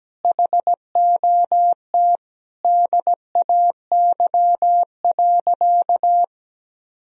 Morse code, heard as a keyed tone, about 17 wpm, 700 Hz. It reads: HOT DAY.
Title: Sentences from Top 1000 Words - Review Part 3 - 17wpm